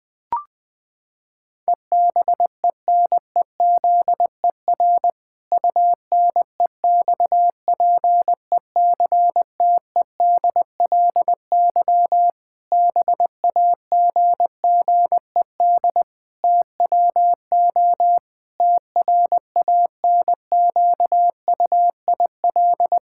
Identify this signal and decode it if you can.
Morse code, heard as a keyed tone, about 19 wpm, 700 Hz. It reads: EBENEZER UNEXPECTEDLY BAGGED TWO TRANQUIL
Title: Pangram Challenge v2 - 20wpm